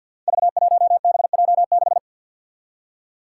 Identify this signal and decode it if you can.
Morse code, heard as a keyed tone, about 46 wpm, 700 Hz. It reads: V1BJ6
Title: Five Challenging Letter-Number Combinations 50wpm